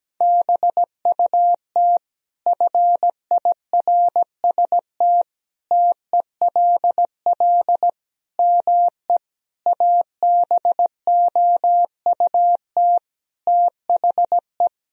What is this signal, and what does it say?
Morse code, heard as a keyed tone, about 17 wpm, 700 Hz. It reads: BUT FIRST TELL ME ABOUT THE
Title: Sentences from Top 1000 Words - Review Part 2 - 17wpm